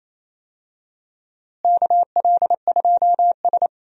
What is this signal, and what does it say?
Morse code, heard as a keyed tone, about 28 wpm, 700 Hz. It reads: KL2S
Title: USA Call Signs - 4 Characters - Repeated 2x - 28wpm